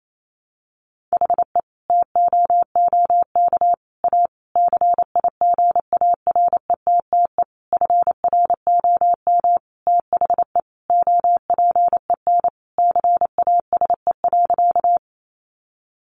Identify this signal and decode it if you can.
Morse code, heard as a keyed tone, about 28 wpm, 700 Hz. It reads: HE TOOK A CIGARETTE FROM THE OPEN CASE.
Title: Sentences from Top 1000 Words 28wpm